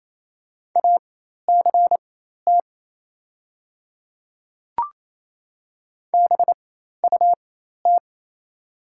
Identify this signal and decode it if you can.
Morse code, heard as a keyed tone, about 27 wpm, spaced as for 13 wpm, 700 Hz. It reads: ACT BUT
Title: 4x Character Spacing - Top 200 Words - 28wpm